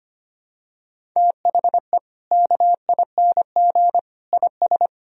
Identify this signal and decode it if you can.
Morse code, heard as a keyed tone, about 25 wpm, 700 Hz. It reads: THE KING IS
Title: Sentences from Top 1000 Words - Review Part 1 - 25wpm